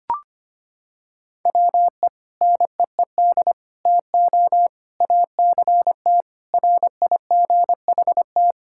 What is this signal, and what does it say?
Morse code, heard as a keyed tone, about 25 wpm, 700 Hz. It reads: WE NEED TO ACT RIGHT
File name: practice / 25wpm / Sentences with 2-Word Phrases - 25wpm